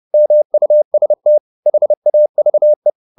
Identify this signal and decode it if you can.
Morse code, heard as a keyed tone, about 30 wpm, 600 Hz. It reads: MUST HAVE
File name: the old man and the sea / the old man and the sea 1 30wpm